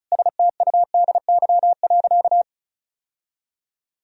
Morse code, heard as a keyed tone, about 36 wpm, 700 Hz. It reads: STUDY.
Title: Sentences from Top 1000 Words - Review Part 1 - 35wpm